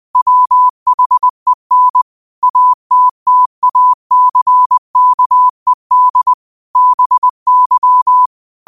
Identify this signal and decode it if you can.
Morse code, heard as a keyed tone, about 20 wpm, 1000 Hz. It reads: WHEN ATTACKED BY
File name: qotdfiles / 20wpm